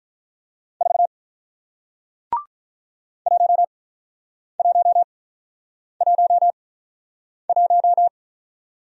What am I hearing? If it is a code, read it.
Morse code, heard as a keyed tone, about 48 wpm, spaced as for 9 wpm, 700 Hz. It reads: 4 1111